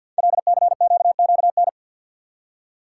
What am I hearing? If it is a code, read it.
Morse code, heard as a keyed tone, about 50 wpm, 700 Hz. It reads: LCQQN
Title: Code Groups - 5 Letters - 50wpm